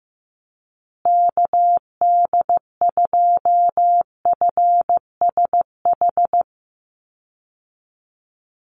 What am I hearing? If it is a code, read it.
Morse code, heard as a keyed tone, about 15 wpm, 700 Hz. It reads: KD2FSH
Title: Call Signs Repeated 2x - 15wpm